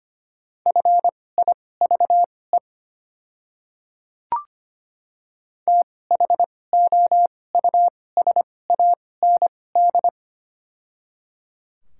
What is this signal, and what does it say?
Morse code, heard as a keyed tone, about 25 wpm, spaced as for 18 wpm, 700 Hz. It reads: FIVE THOUSAND
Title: 2x Character Spacing - Top 500 Words - 25wpm